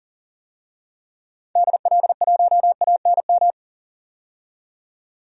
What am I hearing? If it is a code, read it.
Morse code, heard as a keyed tone, about 40 wpm, 700 Hz. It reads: DL1ANM
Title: DXCC-100 Call Signs - 6 Characters - 40wpm